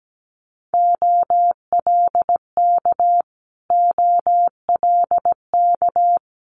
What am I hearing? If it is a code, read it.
Morse code, heard as a keyed tone, about 17 wpm, 700 Hz. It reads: OLK OLK
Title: Multi-letter Phonemes Repeated 3x 17wpm